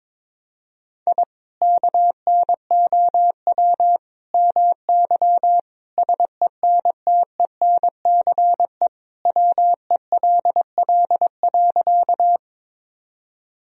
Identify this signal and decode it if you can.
Morse code, heard as a keyed tone, about 22 wpm, 700 Hz. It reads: I KNOW MY SENTENCE WELL.
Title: Sentences from Top 1000 Words - Review Part 1 - 22wpm